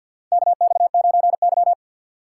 Morse code, heard as a keyed tone, about 47 wpm, 700 Hz. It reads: KX9Y
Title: USA Call Signs - 4 Characters - 3x Character Spacing - 50wpm